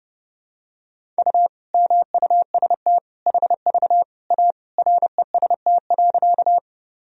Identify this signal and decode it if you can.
Morse code, heard as a keyed tone, about 30 wpm, 700 Hz. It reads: U MUST HV A REST.
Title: Sentences from Top 500 Words and Abbreviations 30wpm